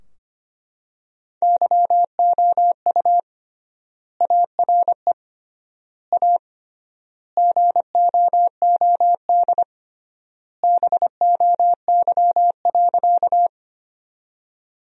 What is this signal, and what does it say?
Morse code, heard as a keyed tone, about 25 wpm, 700 Hz. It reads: YOU ARE A GOOD BOY.
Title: 3x Word-Spacing - Sentences from Top 200 Words - 25wpm